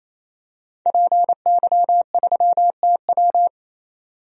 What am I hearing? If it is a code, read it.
Morse code, heard as a keyed tone, about 28 wpm, 700 Hz. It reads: PY3TW
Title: DXCC-100 Call Signs - 5 Characters - 28wpm